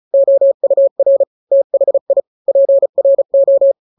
Morse code, heard as a keyed tone, about 34 wpm, 550 Hz. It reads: OUR THI PRO
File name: 3-grams 35 30 WPM